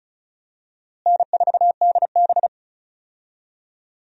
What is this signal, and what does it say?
Morse code, heard as a keyed tone, about 36 wpm, 700 Hz. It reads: N4DB
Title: USA Call Signs - 4 Characters - Repeated 2x - 35wpm